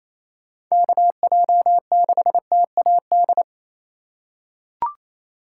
Morse code, heard as a keyed tone, about 28 wpm, 700 Hz. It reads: KJ6TAD
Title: USA Call Signs - 6 Characters - 4x Character Spacing - 28wpm